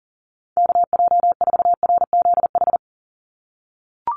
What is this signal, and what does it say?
Morse code, heard as a keyed tone, about 40 wpm, 700 Hz. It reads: KJ4RZH
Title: USA Call Signs 40wpm